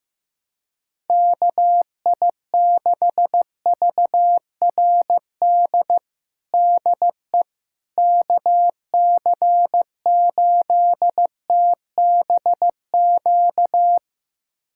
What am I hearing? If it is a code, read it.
Morse code, heard as a keyed tone, about 15 wpm, 700 Hz. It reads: KI6VRD DE KC8TBQ
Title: USA Callsign DE Callsign 15wpm